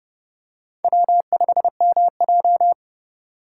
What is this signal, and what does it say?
Morse code, heard as a keyed tone, about 29 wpm, 700 Hz. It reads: W5MJ
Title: USA Call Signs - 4 Characters - 3x Character Spacing - 30wpm